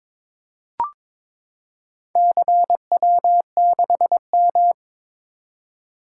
Morse code, heard as a keyed tone, about 22 wpm, 700 Hz. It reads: CW6M
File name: practice / 22wpm / Four Letter-Number Combinations - 22wpm